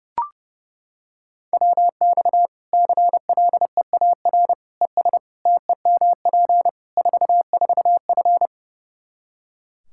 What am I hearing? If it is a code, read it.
Morse code, heard as a keyed tone, about 30 wpm, 700 Hz. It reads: WX CLEAR ES TEMP 44F
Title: Practice QSO - Exchange 2 - Info - 30wpm